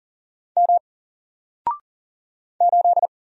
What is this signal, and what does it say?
Morse code, heard as a keyed tone, about 39 wpm, 700 Hz. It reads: M 8
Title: Single Character (Letter:Number:Punctuation:Prosign) Alt 40wpm